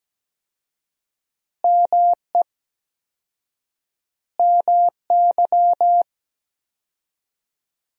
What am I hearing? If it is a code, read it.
Morse code, heard as a keyed tone, about 17 wpm, 700 Hz. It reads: ME MY
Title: 4x Word-Spacing - Sentences from Top 200 Words - 17wpm